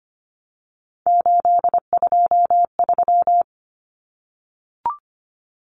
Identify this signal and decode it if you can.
Morse code, heard as a keyed tone, about 25 wpm, 700 Hz. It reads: 823
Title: Common Cut Numbers - 3-Digit 25wpm